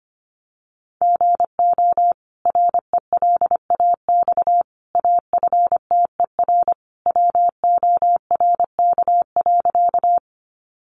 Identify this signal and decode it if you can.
Morse code, heard as a keyed tone, about 25 wpm, 700 Hz. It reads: GO RELAX AFTER WORK.
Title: Sentences from Top 1400 Words 25wpm